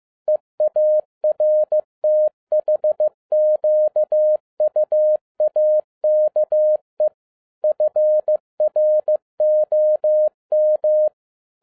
Morse code, heard as a keyed tone, about 15 wpm, 610 Hz. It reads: EARTHQUAKE FROM